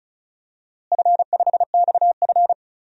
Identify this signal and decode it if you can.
Morse code, heard as a keyed tone, about 36 wpm, 700 Hz. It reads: F5XF
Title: DXCC-100 Call Signs - 4 Characters - Repeated 3x - 35wpm